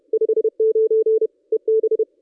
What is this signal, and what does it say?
Morse code, heard as a keyed tone, about 30 wpm, 440 Hz. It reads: 59 EB
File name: contests / ss-cw-k5tr-op-n5rz -1800